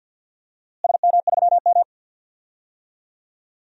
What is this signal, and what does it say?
Morse code, heard as a keyed tone, about 52 wpm, 700 Hz. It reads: SM3M
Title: DXCC-100 Call Signs - 4 Characters - Repeated 2x - 50wpm